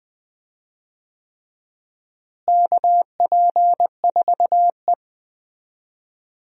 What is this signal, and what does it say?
Morse code, heard as a keyed tone, about 20 wpm, 700 Hz. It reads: KP4E
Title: DXCC-100 Call Signs - 4 Characters - Repeated 3x - 20wpm